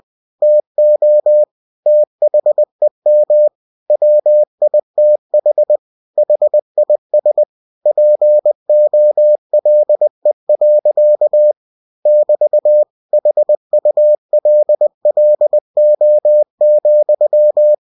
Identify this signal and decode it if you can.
Morse code, heard as a keyed tone, about 20 wpm, 600 Hz. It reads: TO THEM WITH HIS POLE. <BT> HULLO,
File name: cw-books / winnie-the-pooh / winnie-20wpm-page-0088